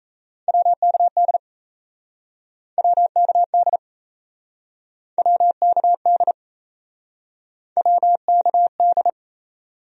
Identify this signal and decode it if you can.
Morse code, heard as a keyed tone, about 38 wpm, 700 Hz. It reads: WKD WKD WKD WKD